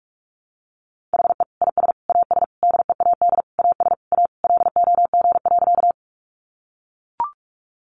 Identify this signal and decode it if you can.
Morse code, heard as a keyed tone, about 44 wpm, 700 Hz. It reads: HE IS AS DEAD AS A LOG.
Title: Sentences from Top 1500 Words 45wpm